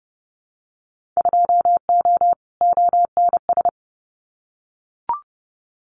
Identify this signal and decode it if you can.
Morse code, heard as a keyed tone, about 30 wpm, 700 Hz. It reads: 2O ONS
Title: Winter Field Day Exchanges Encore 30wpm